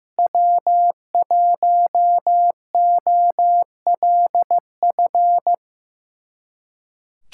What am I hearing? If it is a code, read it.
Morse code, heard as a keyed tone, about 15 wpm, 700 Hz. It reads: W1OLF
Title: USA Callsign DE Callsign - 5 Characters - 15wpm